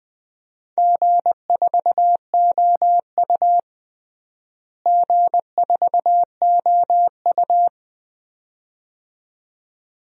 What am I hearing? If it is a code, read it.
Morse code, heard as a keyed tone, about 21 wpm, 700 Hz. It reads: G4OU G4OU
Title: DXCC-100 Call Signs - 4 Characters - Repeated 2x - 20wpm